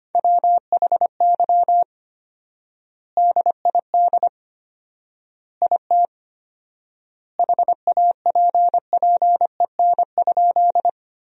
Morse code, heard as a keyed tone, about 25 wpm, 700 Hz. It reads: WHY DID IT HAPPEN?
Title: 4x Word-Spacing - Sentences from Top 400 Words - 25wpm